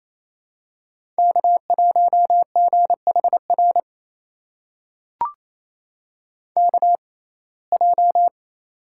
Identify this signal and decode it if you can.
Morse code, heard as a keyed tone, about 28 wpm, spaced as for 20 wpm, 700 Hz. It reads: K1GHR KJ
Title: USA Call Signs - 5 Characters - 6x Character Spacing - 28wpm